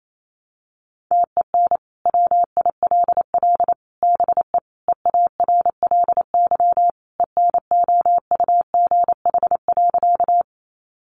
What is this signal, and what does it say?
Morse code, heard as a keyed tone, about 28 wpm, 700 Hz. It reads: TEN WILL BE EARLY ENOUGH.
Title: Sentences from Top 400 Words 28wpm